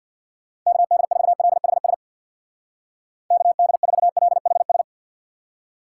Morse code, heard as a keyed tone, about 63 wpm, 700 Hz. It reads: KD4LHS KD4LHS